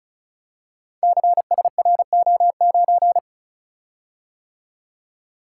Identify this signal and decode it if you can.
Morse code, heard as a keyed tone, about 36 wpm, 700 Hz. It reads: CSRO9
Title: Code Groups - 5 Letter:Numbers - 35wpm